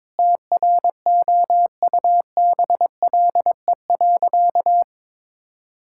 Morse code, heard as a keyed tone, about 22 wpm, 700 Hz. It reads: TROUBLE.
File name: Sentences with 9-Word Phrases - 22wpm